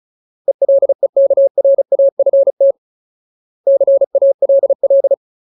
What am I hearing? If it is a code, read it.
Morse code, heard as a keyed tone, about 35 wpm, 550 Hz. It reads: ELEKRAFT CALL
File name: Ultimate Word List / 35WPM Ultimate